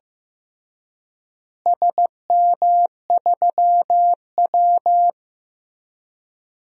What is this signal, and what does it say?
Morse code, heard as a keyed tone, about 15 wpm, 700 Hz. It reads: SM3W
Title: DXCC-100 Call Signs - 4 Characters - Repeated 3x - 15wpm